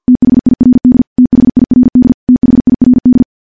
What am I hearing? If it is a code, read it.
Morse code, heard as a keyed tone, about 50 wpm, 250 Hz. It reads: THIRD THIRD THIRD
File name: morse / morse-training-50WPM